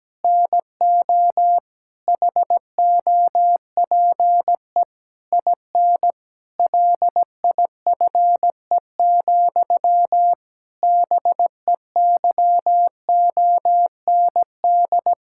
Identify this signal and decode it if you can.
Morse code, heard as a keyed tone, about 17 wpm, 700 Hz. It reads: NO HOPE IN LIFE, BEYOND
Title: Sentences from Top 1000 Words - Review Part 6 - 17wpm